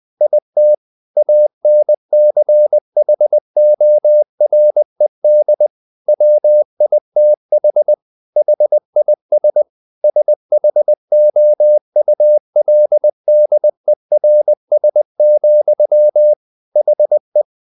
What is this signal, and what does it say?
Morse code, heard as a keyed tone, about 20 wpm, 600 Hz. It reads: IT ANCHORED WITH HIS SHOULDERS, HE
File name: the old man and the sea / the old man and the sea part 1 20wpm